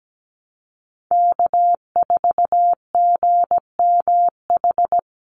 Morse code, heard as a keyed tone, about 17 wpm, 700 Hz. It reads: K4GMH
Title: Call Signs Repeated 2x - 17wpm